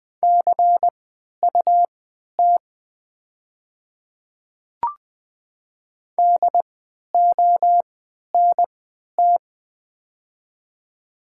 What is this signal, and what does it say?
Morse code, heard as a keyed tone, about 20 wpm, spaced as for 11 wpm, 690 Hz. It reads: CUT DONT
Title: 3x Character Spacing - Top 300 Words - 20wpm